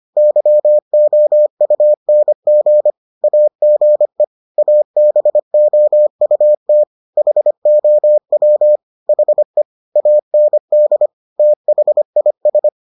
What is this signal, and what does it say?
Morse code, heard as a keyed tone, about 25 wpm, 600 Hz. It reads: YOUNG AGE ABOUT HOW HE AND THIS